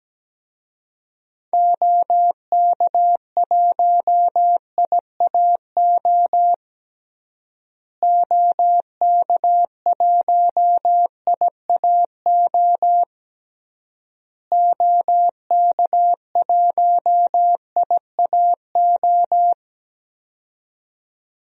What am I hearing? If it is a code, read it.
Morse code, heard as a keyed tone, about 17 wpm, 700 Hz. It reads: OK1IAO OK1IAO OK1IAO
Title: DXCC-100 Call Signs - 6 Characters - Repeated 3x - 17wpm